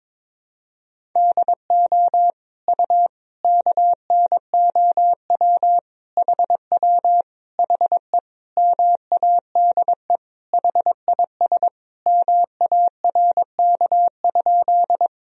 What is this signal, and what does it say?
Morse code, heard as a keyed tone, about 22 wpm, 700 Hz. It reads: DO U KNOW HW HE MADE HIS MARK?
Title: Sentences from Top 300 Words and Abbreviations 22wpm